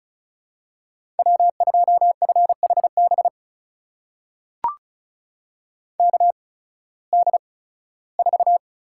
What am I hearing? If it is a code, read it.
Morse code, heard as a keyed tone, about 34 wpm, spaced as for 18 wpm, 700 Hz. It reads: W2FHB KD4
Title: USA Call Signs - 8x Character Spacing - 35wpm